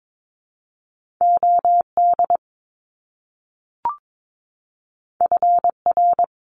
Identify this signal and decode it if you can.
Morse code, heard as a keyed tone, about 22 wpm, 700 Hz. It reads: OD FR
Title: Top Two Letter Combinations in Words 22wpm